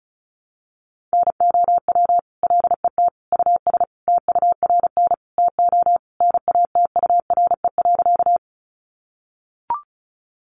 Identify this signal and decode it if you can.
Morse code, heard as a keyed tone, about 34 wpm, 700 Hz. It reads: NOW LET US TURN TO NATURE.
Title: Sentences from Top 600 Words 35wpm